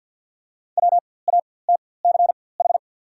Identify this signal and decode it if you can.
Morse code, heard as a keyed tone, about 49 wpm, spaced as for 23 wpm, 690 Hz. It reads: WATCH
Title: 4x Character Spacing - Top 500 Words - 50wpm